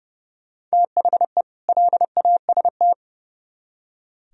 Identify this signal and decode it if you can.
Morse code, heard as a keyed tone, about 34 wpm, 700 Hz. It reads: THE LAST
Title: Speed-Racing - Sets of 2 Words 30wpm